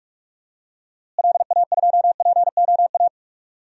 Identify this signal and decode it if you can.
Morse code, heard as a keyed tone, about 46 wpm, 700 Hz. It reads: LA2POA
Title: DXCC-100 Call Signs - 6 Characters - Repeated 2x - 45wpm